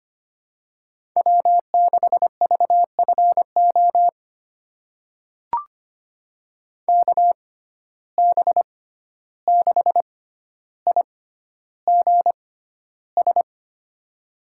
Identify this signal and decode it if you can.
Morse code, heard as a keyed tone, about 25 wpm, spaced as for 12 wpm, 700 Hz. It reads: W6VFO KB6IGS
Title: USA Call Signs - 6x Character Spacing - 25wpm